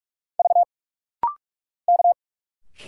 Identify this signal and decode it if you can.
Morse code, heard as a keyed tone, about 45 wpm, 700 Hz. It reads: V K
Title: Single Letters - Rapid-Fire 45wpm